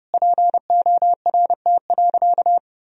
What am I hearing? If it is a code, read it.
Morse code, heard as a keyed tone, about 32 wpm, 700 Hz. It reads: PORT.
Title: Sentences from Top 1000 Words - Review Part 1 - 30wpm